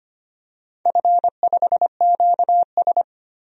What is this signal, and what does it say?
Morse code, heard as a keyed tone, about 25 wpm, 700 Hz. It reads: F5QS
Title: DXCC-100 Call Signs - 4 Characters - 25wpm